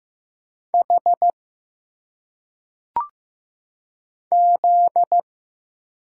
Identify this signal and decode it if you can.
Morse code, heard as a keyed tone, about 15 wpm, 700 Hz. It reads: H Z
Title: Single Letters - Phonetic Pronunciation 15wpm